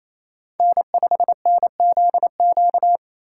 Morse code, heard as a keyed tone, about 27 wpm, 700 Hz. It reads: N5NZQ
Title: USA Call Signs - 5 Characters - 4x Character Spacing - 28wpm